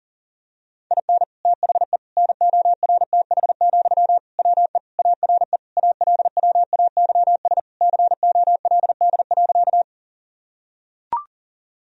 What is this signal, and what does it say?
Morse code, heard as a keyed tone, about 37 wpm, 700 Hz. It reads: IN THE NORTH, WE ARE ALWAYS COLD.
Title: Sentences from Top 1000 Words - Review Part 1 - 40wpm